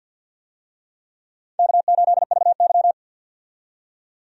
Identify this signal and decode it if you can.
Morse code, heard as a keyed tone, about 50 wpm, 700 Hz. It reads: K8VY
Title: USA Call Signs - 4 Characters - 50wpm